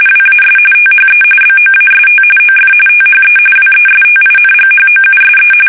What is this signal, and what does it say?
RTTY, about 45.45 baud. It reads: WAS THE BOAST OF THE WESTINGHOUSE
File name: fsk850 / benjamin-g-lamme